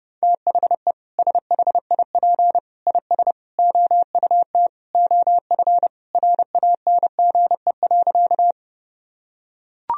Morse code, heard as a keyed tone, about 32 wpm, 700 Hz. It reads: THE SHIP IS OUT OF RANGE.
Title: Sentences from Top 1000 Words - Review Part 4 - 30wpm